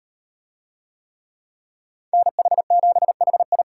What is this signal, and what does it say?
Morse code, heard as a keyed tone, about 38 wpm, 700 Hz. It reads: NH7HI